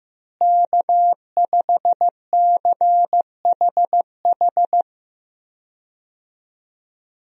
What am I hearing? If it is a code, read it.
Morse code, heard as a keyed tone, about 15 wpm, 700 Hz. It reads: K5CHH
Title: USA Call Signs - 5 Characters - Repeated 3x - 15wpm